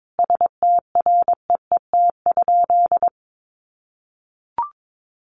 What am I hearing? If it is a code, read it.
Morse code, heard as a keyed tone, about 22 wpm, 700 Hz. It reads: STREET?
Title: Sentences from Top 900 Words 22wpm